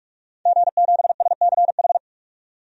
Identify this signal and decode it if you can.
Morse code, heard as a keyed tone, about 41 wpm, 700 Hz. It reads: G7SCH
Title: DXCC-100 Call Signs - 5 Characters - 45wpm